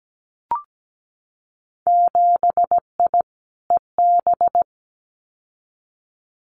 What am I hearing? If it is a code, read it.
Morse code, heard as a keyed tone, about 17 wpm, 700 Hz. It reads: 7I EB
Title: Winter Field Day Exchanges Encore 17wpm